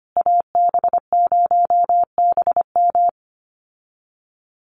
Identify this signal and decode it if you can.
Morse code, heard as a keyed tone, about 25 wpm, 700 Hz. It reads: AB0BM
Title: USA Call Signs Repeated 2x 25wpm